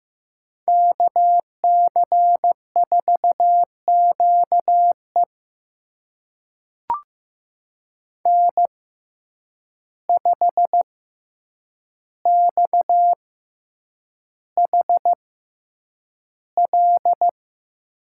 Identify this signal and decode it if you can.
Morse code, heard as a keyed tone, about 15 wpm, spaced as for 8 wpm, 700 Hz. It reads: KC4QE N5XHL
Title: USA Call Signs - 5 Characters - 6x Character Spacing - 15wpm